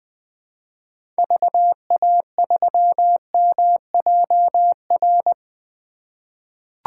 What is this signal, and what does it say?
Morse code, heard as a keyed tone, about 20 wpm, 700 Hz. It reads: VA3MJR
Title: 2023 POTA Hunter Call Signs - 20wpm